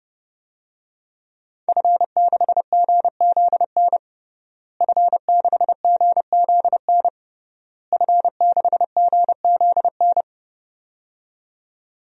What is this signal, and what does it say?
Morse code, heard as a keyed tone, about 30 wpm, 700 Hz. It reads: F6GZN F6GZN F6GZN
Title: DXCC-100 Call Signs - 5 Characters - Repeated 3x - 30wpm